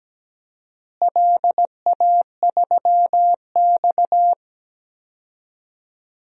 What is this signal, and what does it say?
Morse code, heard as a keyed tone, about 17 wpm, 700 Hz. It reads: LA3X
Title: DXCC-100 Call Signs - 4 Characters - Repeated 2x - 17wpm